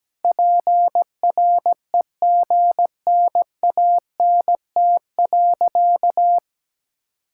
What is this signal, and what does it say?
Morse code, heard as a keyed tone, about 17 wpm, 700 Hz. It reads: PREGNANT.
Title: Sentences from Top 1000 Words - Review Part 6 - 17wpm